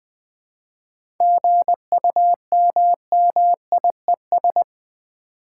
Morse code, heard as a keyed tone, about 20 wpm, 700 Hz. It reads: GUMMIES
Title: Easy Words with Hints - 20wpm